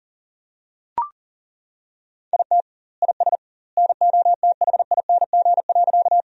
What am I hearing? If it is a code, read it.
Morse code, heard as a keyed tone, about 40 wpm, 700 Hz. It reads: IT IS NOTHING.
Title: 2x Word-Spacing - Sentences from Top 500 Words - 40wpm